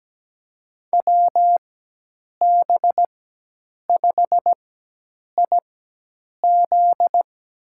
Morse code, heard as a keyed tone, about 17 wpm, spaced as for 8 wpm, 700 Hz. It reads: WB5IZ
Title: USA Call Signs - 4x Character Spacing - 17wpm